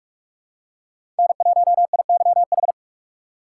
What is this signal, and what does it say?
Morse code, heard as a keyed tone, about 44 wpm, 700 Hz. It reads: N1IYH
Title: USA Call Signs - 5 Characters - 3x Character Spacing - 45wpm